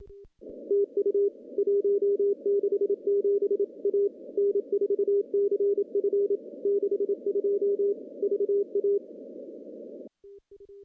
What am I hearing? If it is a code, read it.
Morse code, heard as a keyed tone, about 29 wpm, 400 Hz. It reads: TU 167 A N4CF 62 VA
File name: contests / CW-SS-k5tr- -1500